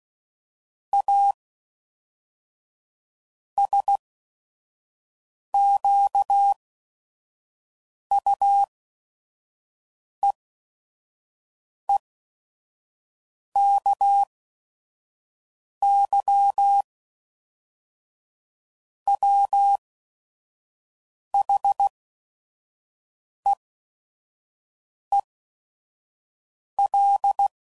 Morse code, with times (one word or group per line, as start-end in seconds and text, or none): 0.9-1.3 A
3.6-16.8 SQUEEKY
19.1-27.5 WHEEL